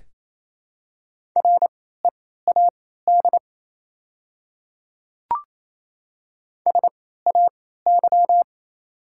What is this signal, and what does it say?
Morse code, heard as a keyed tone, about 27 wpm, spaced as for 16 wpm, 700 Hz. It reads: READ SAY
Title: 3x Character Spacing - Top 200 Words - 28wpm